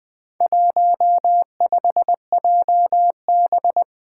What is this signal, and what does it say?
Morse code, heard as a keyed tone, about 20 wpm, 700 Hz. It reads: 15JB
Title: Four Challenging Letter-Number Combinations 20wpm